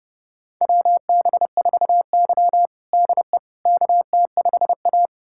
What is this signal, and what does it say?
Morse code, heard as a keyed tone, about 30 wpm, 700 Hz. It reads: WB4Y DE KT5A
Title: USA Callsign DE Callsign - 4 Characters - 30wpm